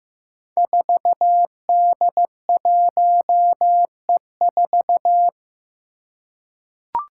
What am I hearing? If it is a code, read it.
Morse code, heard as a keyed tone, about 15 wpm, 700 Hz. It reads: 4D1E4
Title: Five Challenging Letter-Number Combinations 15wpm